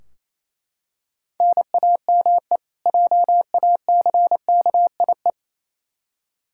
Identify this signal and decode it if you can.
Morse code, heard as a keyed tone, about 28 wpm, 700 Hz. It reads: NAME JACKIE
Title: Practice QSO - Exchange 1 - Beginning - 28wpm